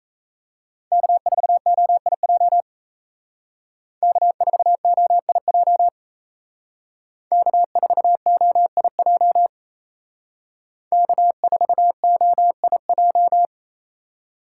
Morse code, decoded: K4OIJ K4OIJ K4OIJ K4OIJ

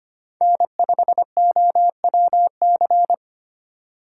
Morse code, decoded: N5OWC

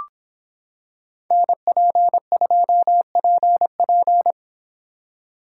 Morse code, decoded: NP2PP

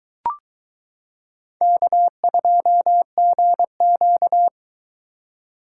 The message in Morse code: K2GQ

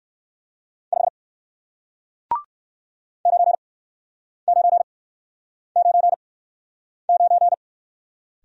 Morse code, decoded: 5 9999

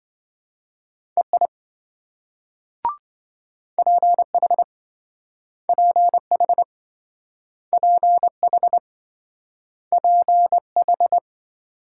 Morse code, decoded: EI PH PH PH PH

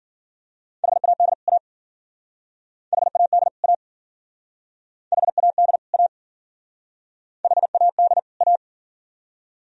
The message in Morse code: HAD A HAD A HAD A HAD A

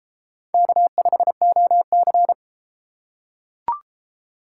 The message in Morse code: K5OC

